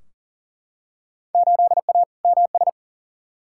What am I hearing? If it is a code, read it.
Morse code, heard as a keyed tone, about 40 wpm, 700 Hz. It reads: 8A MS
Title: Field Day Exchanges - 40wpm